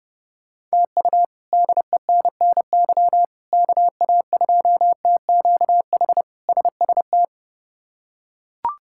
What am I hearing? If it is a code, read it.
Morse code, heard as a keyed tone, about 30 wpm, 700 Hz. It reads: TU DENNY KA2TQH SST
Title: K1USN SST QSO Practice - 30wpm